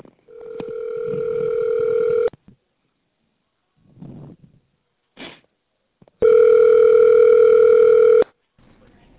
Ringback tone (North America), 440 + 480 Hz.